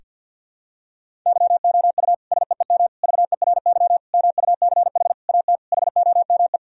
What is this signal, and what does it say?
Morse code, heard as a keyed tone, about 52 wpm, 700 Hz. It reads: YOU SEEM VERY MUCH AT HOME
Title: Sentences from Top 1000 Words - Review Part 1 - 50wpm